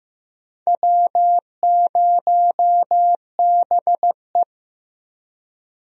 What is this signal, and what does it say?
Morse code, heard as a keyed tone, about 15 wpm, 700 Hz. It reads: W0BE
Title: USA Call Signs - 4 Characters - 15wpm